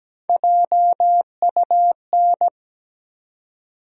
Morse code, heard as a keyed tone, about 17 wpm, 700 Hz. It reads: JUN
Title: USA Call Sign Postfixes - 3 Letters 17wpm